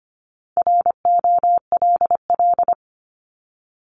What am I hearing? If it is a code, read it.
Morse code, heard as a keyed tone, about 25 wpm, 700 Hz. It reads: ROLL
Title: Top 1751-2000 Words 25wpm